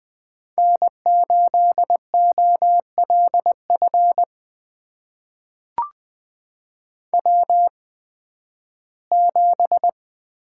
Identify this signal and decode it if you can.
Morse code, heard as a keyed tone, about 20 wpm, spaced as for 13 wpm, 700 Hz. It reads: N8OLF W7
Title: USA Call Signs - 5 Characters - 8x Character Spacing - 20wpm